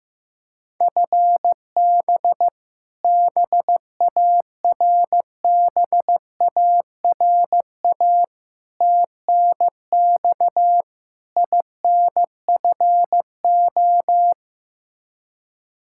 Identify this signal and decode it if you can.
Morse code, heard as a keyed tone, about 15 wpm, 700 Hz. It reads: FB BARBARA TNX INFO
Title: Practice QSO - Exchange 3 - More Info - 15wpm